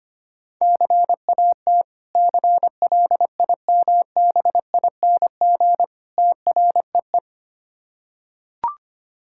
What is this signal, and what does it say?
Morse code, heard as a keyed tone, about 25 wpm, 690 Hz. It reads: CAT CLIMBING TREE
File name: practice / 25wpm / Difficult Words with Hints - 25wpm